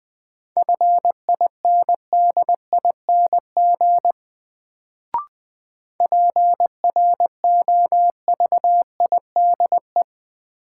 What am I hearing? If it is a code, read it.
Morse code, heard as a keyed tone, about 20 wpm, 700 Hz. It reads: FINDING PROVIDE
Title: Sentences from Top 1100 Words - New Word Drill Alt - 20wpm